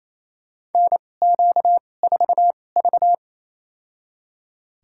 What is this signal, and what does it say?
Morse code, heard as a keyed tone, about 27 wpm, spaced as for 20 wpm, 700 Hz. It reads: NQ4V
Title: USA Call Signs - 2x Character Spacing - 28wpm